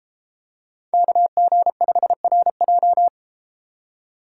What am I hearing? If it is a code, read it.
Morse code, heard as a keyed tone, about 33 wpm, 700 Hz. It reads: KG5RJ